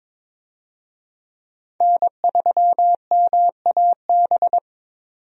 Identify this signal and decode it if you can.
Morse code, heard as a keyed tone, about 22 wpm, 700 Hz. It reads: N3MAB
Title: USA Call Signs - 5 Characters - 22wpm